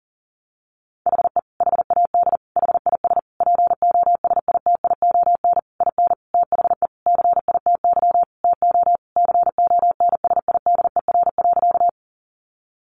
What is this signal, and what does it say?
Morse code, heard as a keyed tone, about 40 wpm, 700 Hz. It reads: HE HAD HIS POSITION IN THE CITY TO CONSIDER.
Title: Sentences from Top 900 Words 40wpm